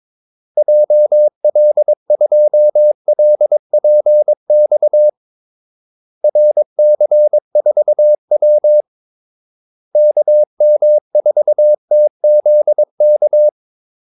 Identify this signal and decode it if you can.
Morse code, heard as a keyed tone, about 22 wpm, 600 Hz. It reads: JL2LPX RC4W KM4TZK
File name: Callsigns-22wpm